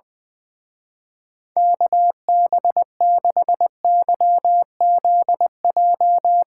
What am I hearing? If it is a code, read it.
Morse code, heard as a keyed tone, about 20 wpm, 700 Hz. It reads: KB6YZJ
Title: Speed Racing - USA 2x3 Call Signs 20wpm